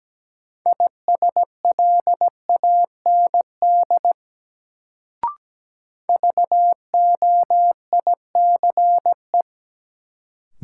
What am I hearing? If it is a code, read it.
Morse code, heard as a keyed tone, about 17 wpm, 700 Hz. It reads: ISLAND VOICE
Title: Sentences from Top 500 Words - New Word Drill - 17wpm